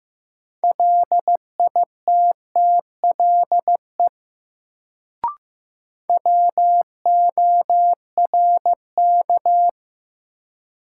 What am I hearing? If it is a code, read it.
Morse code, heard as a keyed tone, about 15 wpm, 700 Hz. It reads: LITTLE WORK
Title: Sentences from Top 200 Words - New Word Drill - 15wpm